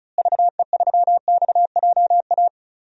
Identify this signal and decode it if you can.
Morse code, heard as a keyed tone, about 36 wpm, 700 Hz. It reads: VE3XJA
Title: DXCC-100 Call Signs - 6 Characters - Repeated 2x - 35wpm